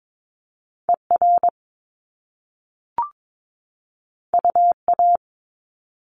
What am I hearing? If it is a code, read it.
Morse code, heard as a keyed tone, about 22 wpm, 700 Hz. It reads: ER UA
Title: Top Two Letter Combinations in Words 22wpm